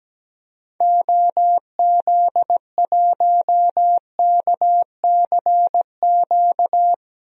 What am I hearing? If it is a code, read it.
Morse code, heard as a keyed tone, about 17 wpm, 700 Hz. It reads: OZ1KCQ
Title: DXCC-100 Call Signs - 6 Characters - 17wpm